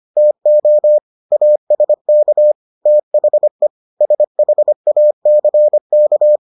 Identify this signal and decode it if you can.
Morse code, heard as a keyed tone, about 25 wpm, 600 Hz. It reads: TO ASK THE SHACK